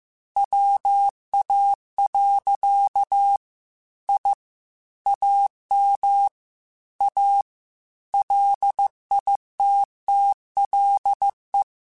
Morse code, heard as a keyed tone, about 15 wpm, 800 Hz. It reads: WA. I AM A LITTLE
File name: QSOs / QSOs-mp3s / WA7QOX 2@13